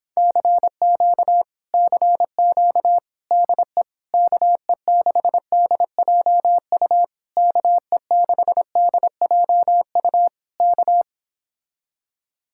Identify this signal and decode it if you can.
Morse code, heard as a keyed tone, about 26 wpm, 700 Hz. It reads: CQ CQ DE KE6DJU KE6DJU K